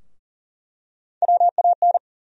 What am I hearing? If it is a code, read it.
Morse code, heard as a keyed tone, about 40 wpm, 700 Hz. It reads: WAN
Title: USA Call Sign Postfixes - 3 Letters 40wpm